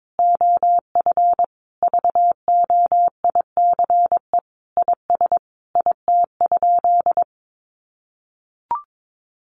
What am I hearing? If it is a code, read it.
Morse code, heard as a keyed tone, about 22 wpm, 700 Hz. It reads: OF VOICE IS IT?
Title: Sentences from Top 500 Words 22wpm